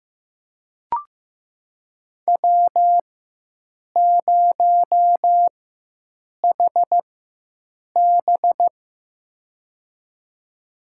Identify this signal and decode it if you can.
Morse code, heard as a keyed tone, about 15 wpm, spaced as for 7 wpm, 700 Hz. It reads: W0HB
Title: USA Call Signs - 4 Characters - 4x Character Spacing - 15wpm